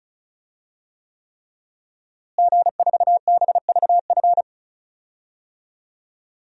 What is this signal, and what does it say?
Morse code, heard as a keyed tone, about 36 wpm, 700 Hz. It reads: G4BVF
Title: DXCC-100 Call Signs - 5 Characters - Repeated 3x - 35wpm